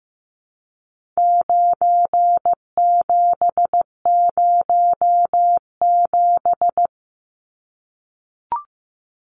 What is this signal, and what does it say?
Morse code, heard as a keyed tone, about 15 wpm, 700 Hz. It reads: 9707
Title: Numbers - 4-Digit 15wpm